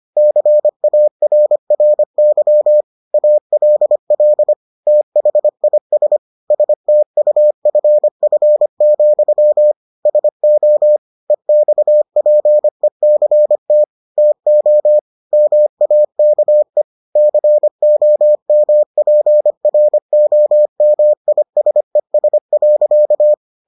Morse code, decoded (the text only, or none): CARRY ALL THIS STUFF, SO EXPECT TO MAKE COMPROMISES.